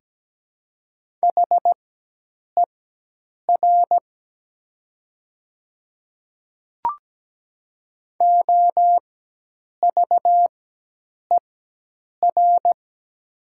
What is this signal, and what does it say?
Morse code, heard as a keyed tone, about 17 wpm, spaced as for 8 wpm, 700 Hz. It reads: HER OVER
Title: 4x Character Spacing - Top 100 Words - 17wpm